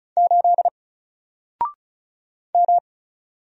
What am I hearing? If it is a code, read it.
Morse code, heard as a keyed tone, about 36 wpm, 700 Hz. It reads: 8 M
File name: Single-Letter-Number-35wpm